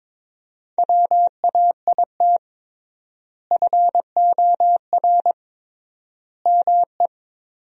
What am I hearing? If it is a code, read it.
Morse code, heard as a keyed tone, about 21 wpm, 700 Hz. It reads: WAIT FOR ME
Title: 3x Word-Spacing - Sentences from Top 500 Words - 22wpm